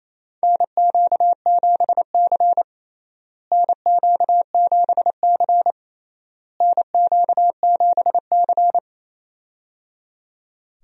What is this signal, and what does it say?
Morse code, heard as a keyed tone, about 27 wpm, 700 Hz. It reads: NQ7C NQ7C NQ7C
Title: USA Call Signs - 4 Characters - Repeated 3x - 28wpm